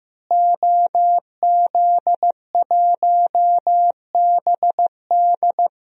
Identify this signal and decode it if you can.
Morse code, heard as a keyed tone, about 15 wpm, 700 Hz. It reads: OZ1BD
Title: DXCC-100 Call Signs - 5 Characters - Repeated 2x - 15wpm